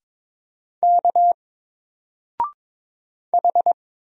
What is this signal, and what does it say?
Morse code, heard as a keyed tone, about 22 wpm, 700 Hz. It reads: K H
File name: Single-Letter-Number-22wpm